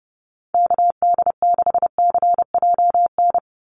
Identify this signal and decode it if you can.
Morse code, heard as a keyed tone, about 30 wpm, 700 Hz. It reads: KD6CJN
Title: USA Call Signs 30wpm